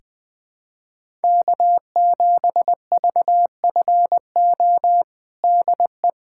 Morse code, heard as a keyed tone, about 20 wpm, 700 Hz. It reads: K7VFO DE
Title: USA Callsign DE Callsign - 5 Characters - 20wpm